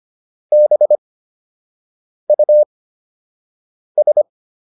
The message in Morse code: BUS

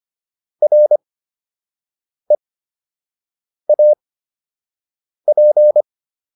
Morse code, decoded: REAP

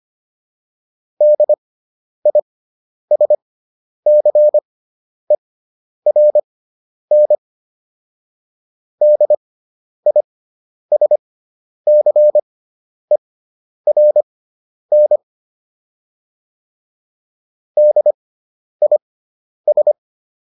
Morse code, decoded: DISCERN DISCERN DIS

